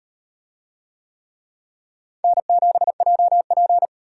Morse code, heard as a keyed tone, about 38 wpm, 700 Hz. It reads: N7JP